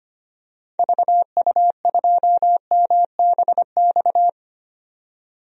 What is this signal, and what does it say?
Morse code, heard as a keyed tone, about 25 wpm, 700 Hz. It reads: VU2MBX
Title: DXCC-100 Call Signs - 6 Characters - 25wpm